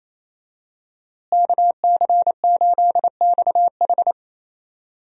Morse code, heard as a keyed tone, about 28 wpm, 700 Hz. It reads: KC8XH
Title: USA Call Signs - 5 Characters - Repeated 3x - 28wpm